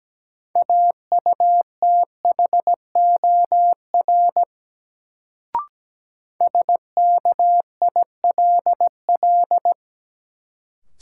Morse code, heard as a keyed tone, about 17 wpm, 700 Hz. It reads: AUTHOR SKILL
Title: Sentences from Top 1200 Words - New Word Drill Alt - 17wpm